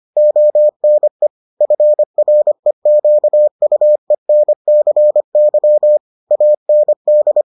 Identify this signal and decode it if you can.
Morse code, heard as a keyed tone, about 25 wpm, 600 Hz. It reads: ONE FREQUENCY AND